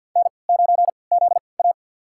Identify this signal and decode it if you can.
Morse code, heard as a keyed tone, about 45 wpm, spaced as for 27 wpm, 700 Hz. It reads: N9ZA
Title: USA Call Signs - 4 Characters - 3x Character Spacing - 50wpm